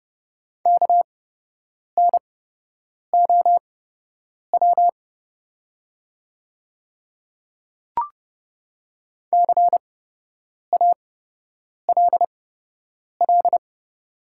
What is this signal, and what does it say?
Morse code, heard as a keyed tone, about 29 wpm, spaced as for 8 wpm, 700 Hz. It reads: KNOW CALL
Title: 8x Character Spacing - Top 100 Words - 30wpm